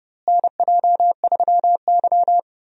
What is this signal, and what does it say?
Morse code, heard as a keyed tone, about 31 wpm, 700 Hz. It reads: NJ3Y
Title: USA Call Signs - 4 Characters - Repeated 3x - 30wpm